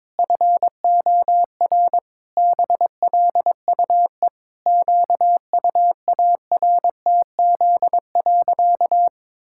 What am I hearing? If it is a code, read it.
Morse code, heard as a keyed tone, about 22 wpm, 700 Hz. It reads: FOR BLUE QUARTZ.
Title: Pangram Challenge v2 - 22wpm